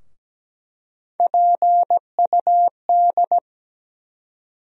Morse code, heard as a keyed tone, about 17 wpm, 700 Hz. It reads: PUD
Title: USA Call Sign Postfixes - 3 Letters 17wpm